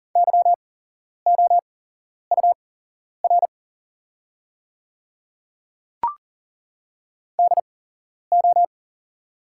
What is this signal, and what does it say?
Morse code, heard as a keyed tone, about 40 wpm, spaced as for 11 wpm, 690 Hz. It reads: YOUR DO
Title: 8x Character Spacing - Top 200 Words - 40wpm